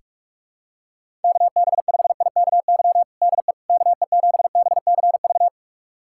Morse code, heard as a keyed tone, about 46 wpm, 700 Hz. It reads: KB5IKY DE KE7BCV
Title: USA Callsign DE Callsign - 6 Characters - 45wpm